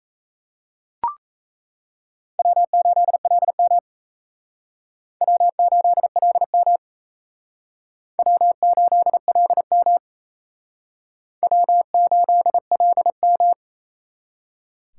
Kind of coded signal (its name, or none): Morse code